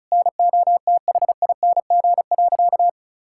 Morse code, heard as a keyed tone, about 36 wpm, 700 Hz. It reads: NOTHING.
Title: Sentences from Top 1000 Words - Review Part 6 - 35wpm